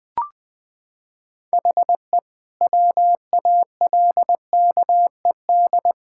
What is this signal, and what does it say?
Morse code, heard as a keyed tone, about 21 wpm, 700 Hz. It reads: HE WALKED
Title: Sentences with 5-Word Phrases - 20wpm